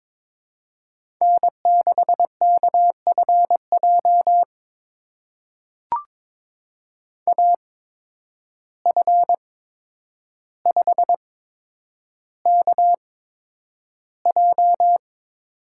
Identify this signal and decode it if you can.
Morse code, heard as a keyed tone, about 22 wpm, spaced as for 9 wpm, 700 Hz. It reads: N6KFJ AF5KJ
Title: USA Call Signs - 5 Characters - 8x Character Spacing - 22wpm